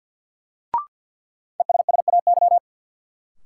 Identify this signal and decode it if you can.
Morse code, heard as a keyed tone, about 53 wpm, 700 Hz. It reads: ESSAY
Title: Sentences from Top 1500 Words - New Word Drill Alt - 50wpm